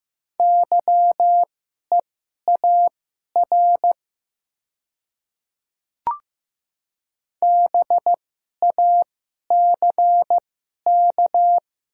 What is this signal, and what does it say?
Morse code, heard as a keyed tone, about 15 wpm, spaced as for 11 wpm, 700 Hz. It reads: YEAR BACK
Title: 2x Character Spacing - Top 300 Words - 15wpm